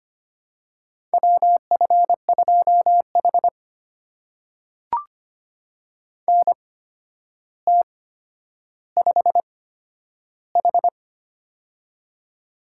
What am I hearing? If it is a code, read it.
Morse code, heard as a keyed tone, about 25 wpm, spaced as for 11 wpm, 700 Hz. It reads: WF2H NT5H
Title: USA Call Signs - 4 Characters - 8x Character Spacing - 25wpm